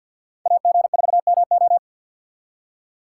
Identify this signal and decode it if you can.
Morse code, heard as a keyed tone, about 53 wpm, 700 Hz. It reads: AG4MO
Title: USA Call Signs - 5 Characters - 6x Character Spacing - 50wpm